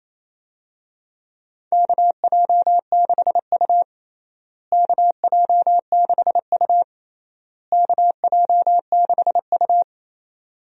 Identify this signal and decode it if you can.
Morse code, heard as a keyed tone, about 28 wpm, 700 Hz. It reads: KJ6U KJ6U KJ6U
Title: USA Call Signs - 4 Characters - Repeated 3x - 28wpm